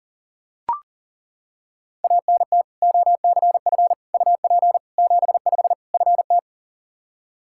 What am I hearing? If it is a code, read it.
Morse code, heard as a keyed tone, about 37 wpm, 700 Hz. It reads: ANT OCF UP 75 FT
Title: Practice QSO - Exchange 2 - Info - 40wpm